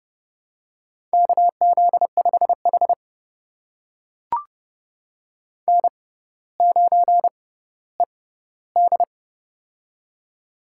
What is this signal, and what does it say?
Morse code, heard as a keyed tone, about 30 wpm, spaced as for 15 wpm, 700 Hz. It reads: KZ5H N9ED
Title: USA Call Signs - 4 Characters - 6x Character Spacing - 30wpm